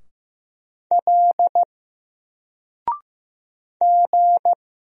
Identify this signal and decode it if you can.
Morse code, heard as a keyed tone, about 15 wpm, 700 Hz. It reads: L G
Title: Single Letter-Number 15wpm